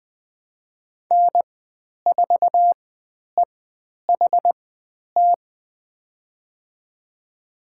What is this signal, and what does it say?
Morse code, heard as a keyed tone, about 20 wpm, spaced as for 10 wpm, 700 Hz. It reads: N4EHT
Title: 095 Lesson 16g - Letter c - Call Signs 20wpm